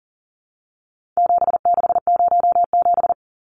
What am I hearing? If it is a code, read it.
Morse code, heard as a keyed tone, about 40 wpm, 700 Hz. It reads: 7607
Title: Numbers - 4-Digit 40wpm